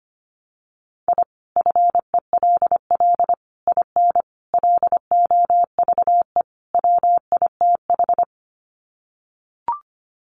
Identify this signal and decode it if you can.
Morse code, heard as a keyed tone, about 25 wpm, 700 Hz. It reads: I FELL IN LOVE WITH
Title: Sets of 5 Words Encore 25wpm